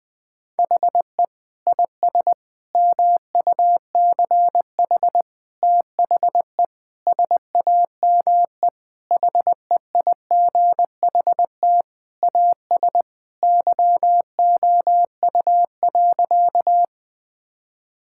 Morse code, heard as a keyed tone, about 20 wpm, 700 Hz. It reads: HE IS MUCH THE SAME HEIGHT AS YOU.
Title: Sentences from Top 1600 Words 20wpm